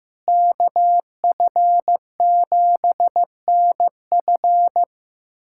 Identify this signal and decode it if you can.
Morse code, heard as a keyed tone, about 15 wpm, 700 Hz. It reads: KF7NF
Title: USA Callsign DE Callsign 15wpm